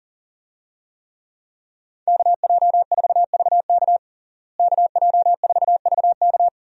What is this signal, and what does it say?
Morse code, heard as a keyed tone, about 40 wpm, 700 Hz. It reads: KJ4VK KJ4VK
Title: USA Call Signs - 5 Characters - Repeated 3x - 40wpm